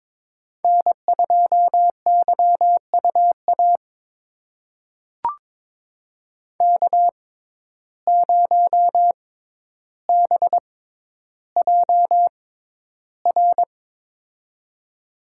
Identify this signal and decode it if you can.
Morse code, heard as a keyed tone, about 22 wpm, spaced as for 11 wpm, 700 Hz. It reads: N2YUA K0BJR
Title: USA Call Signs - 5 Characters - 6x Character Spacing - 22wpm